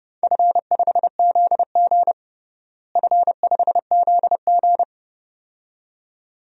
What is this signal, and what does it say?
Morse code, heard as a keyed tone, about 32 wpm, 700 Hz. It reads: F5ZG F5ZG
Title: DXCC-100 Call Signs - 4 Characters - Repeated 2x - 30wpm